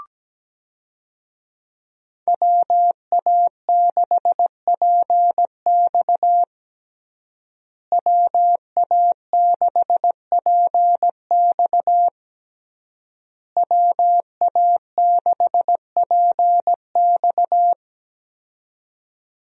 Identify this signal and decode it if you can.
Morse code, heard as a keyed tone, about 17 wpm, 700 Hz. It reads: WA6PX WA6PX WA6PX
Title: USA Call Signs - 5 Characters - Repeated 3x - 17wpm